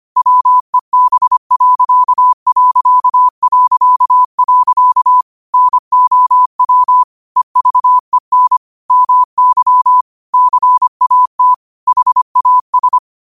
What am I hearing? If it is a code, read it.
Morse code, heard as a keyed tone, about 25 wpm, 1000 Hz. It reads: WEB.... NOW EVEN MY CAT HAS
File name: qotdfiles / 25wpm